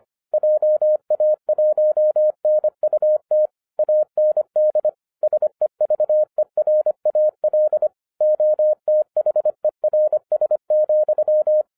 Morse code, heard as a keyed tone, about 25 wpm, 610 Hz. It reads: JA1NUT AND SEVERAL OTHERS,